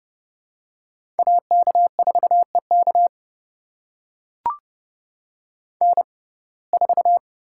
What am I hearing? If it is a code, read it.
Morse code, heard as a keyed tone, about 32 wpm, spaced as for 22 wpm, 700 Hz. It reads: AK4EK N4
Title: USA Call Signs - 5 Characters - 6x Character Spacing - 30wpm